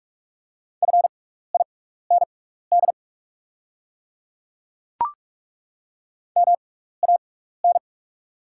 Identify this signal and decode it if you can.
Morse code, heard as a keyed tone, about 43 wpm, spaced as for 15 wpm, 690 Hz. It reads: FIND MAN
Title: 6x Character Spacing - Top 200 Words - 45wpm